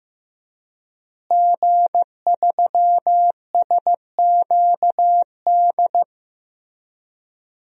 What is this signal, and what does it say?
Morse code, heard as a keyed tone, about 15 wpm, 700 Hz. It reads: G3SQD
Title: DXCC-100 Call Signs - 5 Characters - Repeated 3x - 15wpm